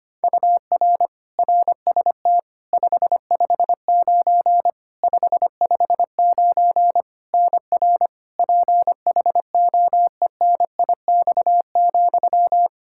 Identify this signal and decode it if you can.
Morse code, heard as a keyed tone, about 25 wpm, 700 Hz. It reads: UR RST 559 559 NR PHOENIX,
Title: Practice QSO - Exchange 1 - Beginning - 25wpm